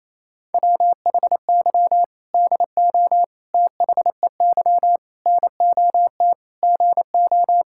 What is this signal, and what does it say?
Morse code, heard as a keyed tone, about 28 wpm, 700 Hz. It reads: WHY DO THEY NOT GO
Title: Sentences from Top 1600 Words 28wpm